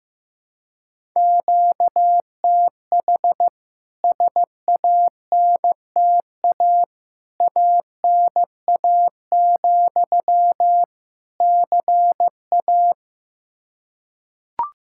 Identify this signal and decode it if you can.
Morse code, heard as a keyed tone, about 15 wpm, 700 Hz. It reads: QTH SANTA ANA, CA
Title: K1USN SST QSO Practice 15wpm